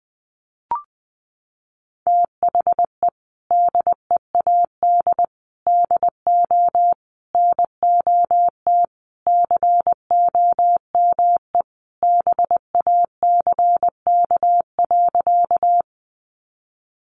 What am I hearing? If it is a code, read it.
Morse code, heard as a keyed tone, about 20 wpm, 700 Hz. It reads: THE DEAD DO NOT COME BACK.
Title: Sentences from Top 1500 Words 20wpm